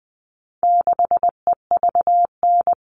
Morse code, heard as a keyed tone, about 20 wpm, 700 Hz. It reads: 6EVN
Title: All Cut Numbers - 4-Digit 20wpm